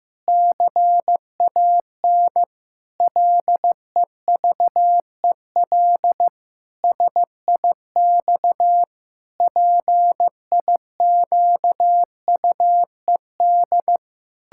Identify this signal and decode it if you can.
Morse code, heard as a keyed tone, about 16 wpm, 700 Hz. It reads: CAN LEVEL SIX PIQUED
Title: Pangram Challenge v2 - 15wpm